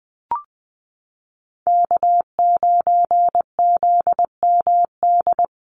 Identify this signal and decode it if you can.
Morse code, heard as a keyed tone, about 20 wpm, 690 Hz. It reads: K9ZMD
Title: NA SOTA Chaser Callsigns 20wpm